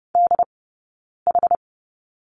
Morse code, heard as a keyed tone, about 30 wpm, spaced as for 9 wpm, 690 Hz. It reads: DH
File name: Lesson 18d - Letter m - Two Characters 30wpm